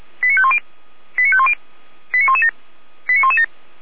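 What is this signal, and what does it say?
Five-tone selective call (ZVEI), heard as quick runs of five tones, tones of about 70 ms. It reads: 89510 89510 89117 89117
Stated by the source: ZVEI five-tone selective call, code 89094, received by SDR.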